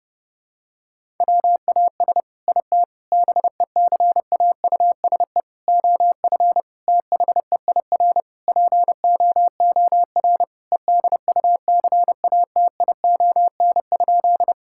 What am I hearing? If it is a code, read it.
Morse code, heard as a keyed tone, about 32 wpm, 700 Hz. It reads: WAS IT BECAUSE OF THEIR POOR EDUCATION?
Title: Sentences from Top 1000 Words - Review Part 6 - 30wpm